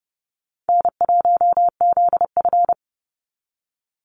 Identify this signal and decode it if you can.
Morse code, heard as a keyed tone, about 30 wpm, 700 Hz. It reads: N1ZF
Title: NA SOTA Activator Callsigns 30wpm